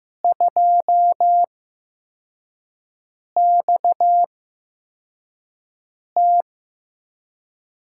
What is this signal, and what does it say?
Morse code, heard as a keyed tone, about 15 wpm, spaced as for 4 wpm, 700 Hz. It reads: 2XT
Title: USA Call Signs - 4 Characters - 8x Character Spacing - 15wpm